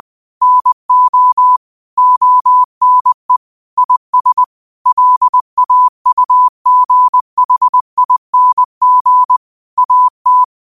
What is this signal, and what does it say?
Morse code, heard as a keyed tone, about 20 wpm, 1000 Hz. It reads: NO ONE IS LAUGHING AT
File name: qotdfiles / 20wpm